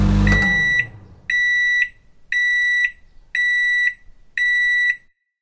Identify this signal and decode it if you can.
There is no code.